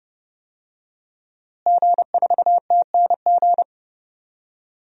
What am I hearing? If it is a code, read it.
Morse code, heard as a keyed tone, about 32 wpm, 700 Hz. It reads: G4TNG
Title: DXCC-100 Call Signs - 5 Characters - Repeated 2x - 30wpm